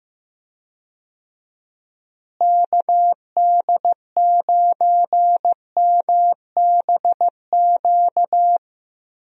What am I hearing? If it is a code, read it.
Morse code, heard as a keyed tone, about 15 wpm, 700 Hz. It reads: KD9MBQ
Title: USA Call Signs - 6 Characters - Repeated 2x - 15wpm